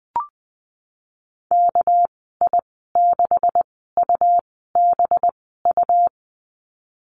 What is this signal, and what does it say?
Morse code, heard as a keyed tone, about 20 wpm, spaced as for 15 wpm, 700 Hz. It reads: KI6UBU
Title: 200 Lesson 32g - Letter k - Call Signs 20wpm